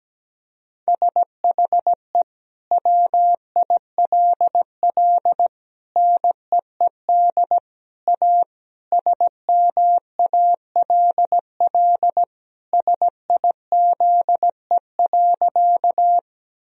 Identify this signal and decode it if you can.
Morse code, heard as a keyed tone, about 17 wpm, 700 Hz. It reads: SHE WILL NEED A SMALL SIZE.
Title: Sentences from Top 1000 Words - Review Part 2 - 17wpm